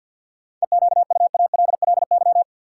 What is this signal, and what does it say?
Morse code, heard as a keyed tone, about 52 wpm, 700 Hz. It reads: EQUALLY
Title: Sentences from Top 1400 Words - New Word Drill Alt - 50wpm